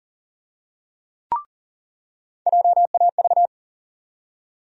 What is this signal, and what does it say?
Morse code, heard as a keyed tone, about 40 wpm, 700 Hz. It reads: JAV
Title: USA Call Sign Postfixes - 3 Letters 40wpm